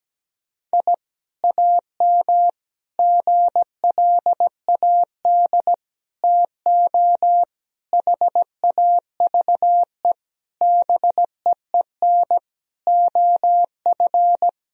Morse code, heard as a keyed tone, about 17 wpm, 700 Hz. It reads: I AM GLAD TO HAVE BEEN OF
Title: Sentences from Top 1000 Words - Review Part 4 - 17wpm